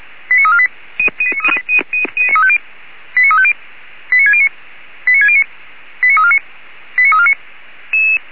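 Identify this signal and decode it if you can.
Five-tone selective call (ZVEI), heard as quick runs of five tones, tones of about 80 ms. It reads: 89348 89350 89609 89609 89349 89349
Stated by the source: Five-tone ZVEI selective call received by SDR.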